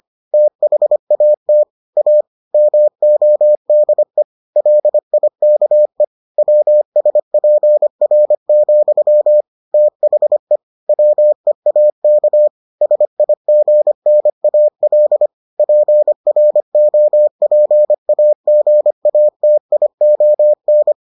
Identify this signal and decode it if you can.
Morse code, heard as a keyed tone, about 25 wpm, 600 Hz. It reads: THAT A MODE LIKE WSPR, THE WEAK SIGNAL PROPAGATION